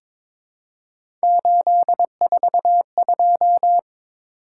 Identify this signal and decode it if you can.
Morse code, heard as a keyed tone, about 22 wpm, 700 Hz. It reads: 842
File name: practice / 22wpm / Three Letter-Number Combinations - 22wpm